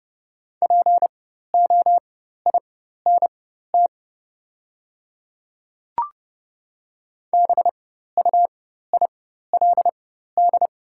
Morse code, heard as a keyed tone, about 30 wpm, spaced as for 14 wpm, 700 Hz. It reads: POINT BUILD
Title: 4x Character Spacing - Top 200 Words - 30wpm